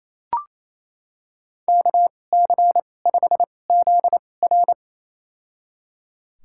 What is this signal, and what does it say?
Morse code, heard as a keyed tone, about 28 wpm, spaced as for 20 wpm, 700 Hz. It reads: KC5ZR
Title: USA Call Signs - 5 Characters - 2x Character Spacing - 28wpm